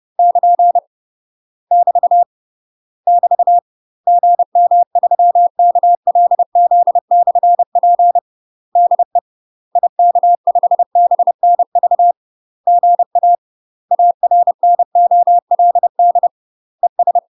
Morse code, heard as a keyed tone, about 30 wpm, 700 Hz. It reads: <KN> <BT> <BT> GM3KLZ/P DE IK5BNV GA ARNOLD ES
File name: UK MASTER QSOs TRAINING / 30WPM / UK MASTER QSOs 0004